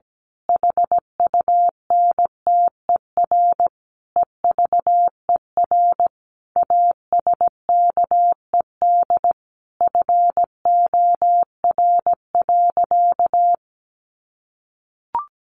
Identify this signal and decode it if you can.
Morse code, heard as a keyed tone, about 17 wpm, 700 Hz. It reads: HUNTER EVER ASKED FOR.
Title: A Sound Of Thunder 17wpm